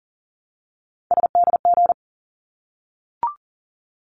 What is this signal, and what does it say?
Morse code, heard as a keyed tone, about 40 wpm, 700 Hz. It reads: SDG
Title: ARRL Sections 40wpm